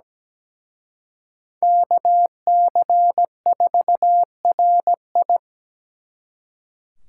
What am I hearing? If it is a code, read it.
Morse code, heard as a keyed tone, about 20 wpm, 700 Hz. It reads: KC4RI
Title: Speed Racing - USA 2x2 Call Signs 17wpm